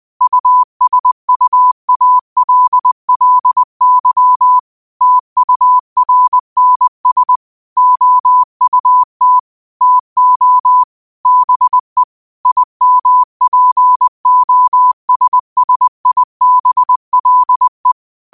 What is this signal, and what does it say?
Morse code, heard as a keyed tone, about 20 wpm, 1000 Hz. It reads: USUALLY TURNS OUT TO BE IMPOSSIBLE